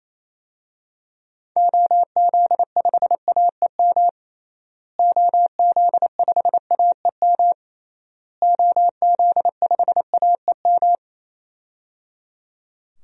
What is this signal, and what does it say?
Morse code, heard as a keyed tone, about 28 wpm, 700 Hz. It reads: OZ5AEM OZ5AEM OZ5AEM
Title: DXCC-100 Call Signs - 6 Characters - Repeated 3x - 28wpm